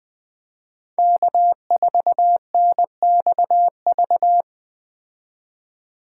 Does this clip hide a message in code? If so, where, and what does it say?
Yes: Morse code from 1.0 to 4.4 s, heard as a keyed tone, about 19 wpm, 700 Hz. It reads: K4NXV